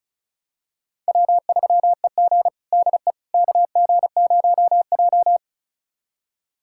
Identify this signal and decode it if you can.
Morse code, heard as a keyed tone, about 34 wpm, 700 Hz. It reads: W3EG DE KG0J
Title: USA Callsign DE Callsign - 4 Characters - 35wpm